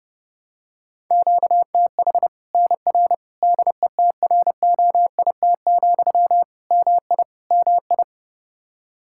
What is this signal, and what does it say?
Morse code, heard as a keyed tone, about 30 wpm, 700 Hz. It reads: QTH NR DETROIT, MI MI
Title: Practice QSO - Exchange 1 - Beginning - 30wpm